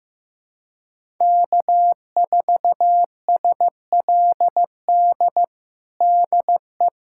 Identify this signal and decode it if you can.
Morse code, heard as a keyed tone, about 16 wpm, 700 Hz. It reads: K4SLD DE
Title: USA Callsign DE Callsign - 5 Characters - 15wpm